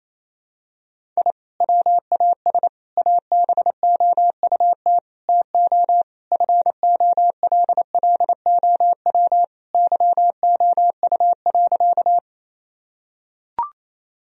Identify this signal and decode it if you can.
Morse code, heard as a keyed tone, about 28 wpm, 700 Hz. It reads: I WAS ABOUT TO FOLLOW YOU.
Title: Sentences from Top 1000 Words - Review Part 1 - 28wpm